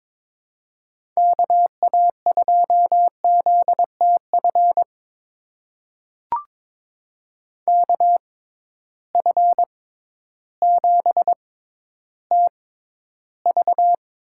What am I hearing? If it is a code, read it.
Morse code, heard as a keyed tone, about 22 wpm, spaced as for 12 wpm, 700 Hz. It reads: KA2ZTF KF7TV
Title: USA Call Signs - 6 Characters - 6x Character Spacing - 22wpm